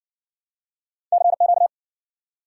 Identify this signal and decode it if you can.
Morse code, heard as a keyed tone, about 60 wpm, 700 Hz. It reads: CQ